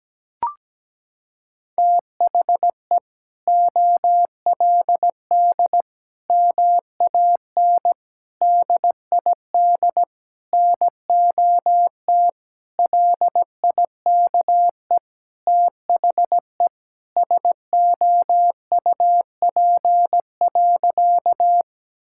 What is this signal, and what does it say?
Morse code, heard as a keyed tone, about 17 wpm, 700 Hz. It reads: THE OLD MAN DID NOT LIKE THE SOUP.
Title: Sentences from Top 1000 Words - Review Part 6 - 17wpm